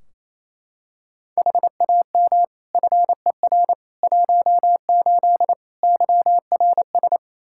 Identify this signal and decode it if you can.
Morse code, heard as a keyed tone, about 28 wpm, 700 Hz. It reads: HAM FER 18 YRS
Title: Practice QSO - Exchange 3 - More Info - 28wpm